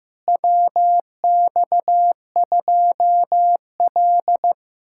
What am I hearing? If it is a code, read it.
Morse code, heard as a keyed tone, about 15 wpm, 700 Hz. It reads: WX2L
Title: USA Callsign DE Callsign - 4 Characters - 15wpm